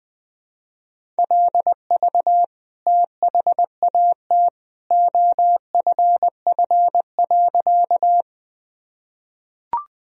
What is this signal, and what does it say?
Morse code, heard as a keyed tone, about 20 wpm, 700 Hz. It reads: LV THAT OFF.
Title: Sentences from Top 400 Words and Abbreviations 20wpm